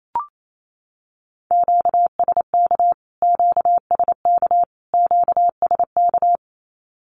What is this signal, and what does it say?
Morse code, heard as a keyed tone, about 28 wpm, 700 Hz. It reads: QSK QSK QSK
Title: Q-Codes Repeated 3x 28wpm